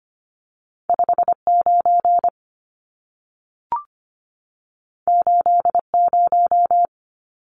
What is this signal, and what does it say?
Morse code, heard as a keyed tone, about 25 wpm, 700 Hz. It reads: 59 80
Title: Numbers - 2-Digit 25wpm